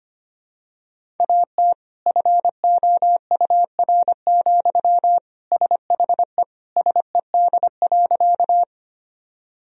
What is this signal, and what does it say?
Morse code, heard as a keyed tone, about 25 wpm, 700 Hz. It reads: AT FOUR, SHE SED.
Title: Sentences from Top 300 Words and Abbreviations 25wpm